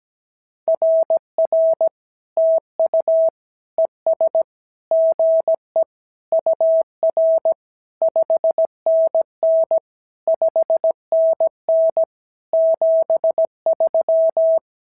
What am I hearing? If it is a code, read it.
Morse code, heard as a keyed tone, about 17 wpm, 650 Hz. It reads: RR TU ES GE UR 5NN 5NN 73
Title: SOTA QSO Practice 17wpm